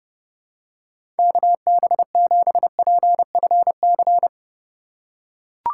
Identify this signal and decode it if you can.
Morse code, heard as a keyed tone, about 30 wpm, 700 Hz. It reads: KB7PFC
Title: USA Call Signs - 6 Characters - 2x Character Spacing - 30wpm